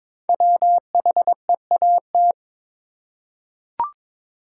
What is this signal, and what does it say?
Morse code, heard as a keyed tone, about 22 wpm, 700 Hz. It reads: WHEAT
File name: 5 Letter Words - First 3 Letters Given - 22wpm